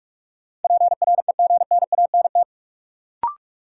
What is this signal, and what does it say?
Morse code, heard as a keyed tone, about 46 wpm, 700 Hz. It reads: PREGNANT
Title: Sentences from Top 1000 Words - Review Part 6 - New Word Drill - 45wpm